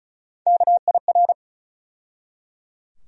Morse code, heard as a keyed tone, about 34 wpm, 700 Hz. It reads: KIR
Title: USA Call Sign Postfixes - 3 Letters 35wpm